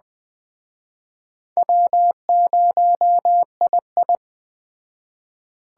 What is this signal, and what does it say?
Morse code, heard as a keyed tone, about 20 wpm, 700 Hz. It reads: W0II
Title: Speed Racing - USA 1x2 Call Signs 17wpm